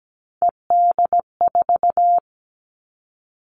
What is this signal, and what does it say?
Morse code, heard as a keyed tone, about 17 wpm, 700 Hz. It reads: ED4
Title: All Cut Numbers - 3-Digit 17wpm